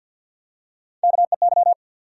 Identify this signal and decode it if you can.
Morse code, heard as a keyed tone, about 54 wpm, 700 Hz. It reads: KEY
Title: Top 180 QSO Elements 50wpm